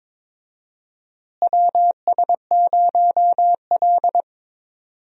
Morse code, heard as a keyed tone, about 22 wpm, 700 Hz. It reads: WS0L
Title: USA Call Signs - 4 Characters - Repeated 3x - 22wpm